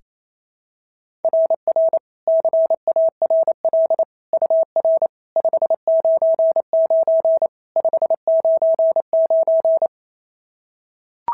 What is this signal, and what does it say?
Morse code, heard as a keyed tone, about 27 wpm, 650 Hz. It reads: RR CARL UR 599 599
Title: SOTA QSO Practice 28wpm